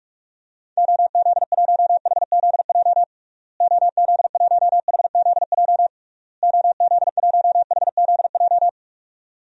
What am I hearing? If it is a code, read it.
Morse code, heard as a keyed tone, about 41 wpm, 700 Hz. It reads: OZ1HZJ OZ1HZJ OZ1HZJ
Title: DXCC-100 Call Signs - 6 Characters - Repeated 3x - 45wpm